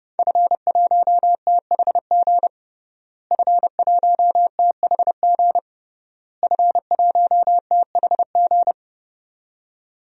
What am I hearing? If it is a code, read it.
Morse code, heard as a keyed tone, about 30 wpm, 700 Hz. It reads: F1THG F1THG F1THG
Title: DXCC-100 Call Signs - 5 Characters - Repeated 3x - 30wpm